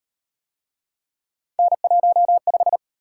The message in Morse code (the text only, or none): N15